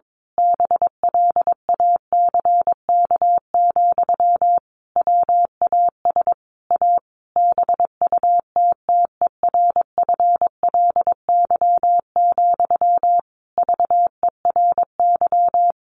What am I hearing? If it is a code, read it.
Morse code, heard as a keyed tone, about 22 wpm, 700 Hz. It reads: BLACK, WAS A BUTTERFLY, VERY
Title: A Sound Of Thunder 22wpm